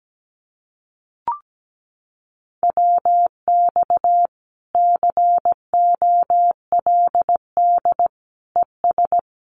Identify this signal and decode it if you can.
Morse code, heard as a keyed tone, about 17 wpm, 700 Hz. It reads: WX COLD ES
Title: Common QSO Phrases 17wpm